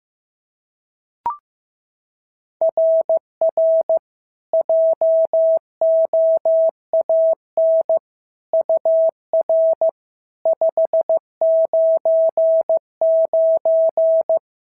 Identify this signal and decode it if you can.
Morse code, heard as a keyed tone, about 14 wpm, 650 Hz. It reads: RR JOAN UR 599
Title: SOTA QSO Practice 15wpm